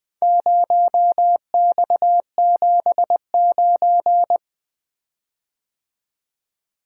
Morse code, heard as a keyed tone, about 22 wpm, 700 Hz. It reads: 0X79
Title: Speed-Racing - Four Letter-Number Combinations - 20wpm